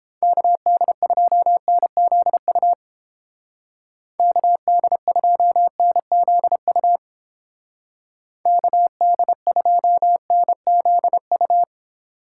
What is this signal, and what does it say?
Morse code, heard as a keyed tone, about 29 wpm, 700 Hz. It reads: KD2NZU KD2NZU KD2NZU